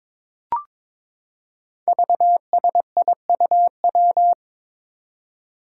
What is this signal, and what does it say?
Morse code, heard as a keyed tone, about 22 wpm, 700 Hz. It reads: VSIUW
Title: Five Challenging Letter-Number Combinations 22wpm